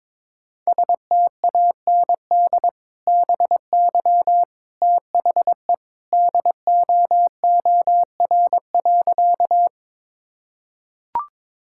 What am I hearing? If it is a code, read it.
Morse code, heard as a keyed tone, about 21 wpm, 700 Hz. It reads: STAND BY THE DOOR.
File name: practice / 22wpm / Sentences with 2-Word Phrases - 22wpm